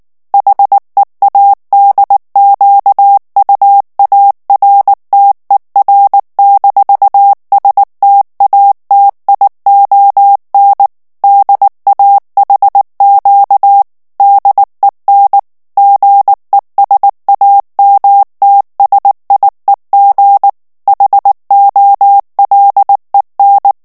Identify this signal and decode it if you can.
Morse code, heard as a keyed tone, about 19 wpm, 800 Hz. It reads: HEADQUARTER-STATION DAHQ DEN GESAMTSIEG HOLEN
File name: morse mp3 / ALLEMAND texte clair 2 vitesse 18 mots minute